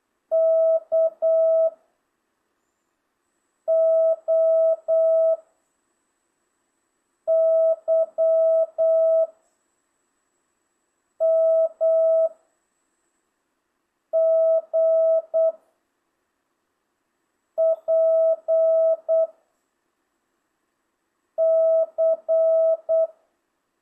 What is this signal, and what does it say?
Morse code, heard as a keyed tone, about 8 wpm, 660 Hz. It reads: K O Y M G P C